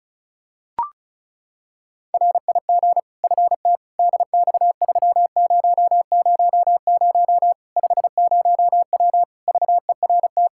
Morse code, heard as a keyed tone, about 36 wpm, 700 Hz. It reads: RIG FT DX3000 50W VERT
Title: Practice QSO - Exchange 2 - Info - 35wpm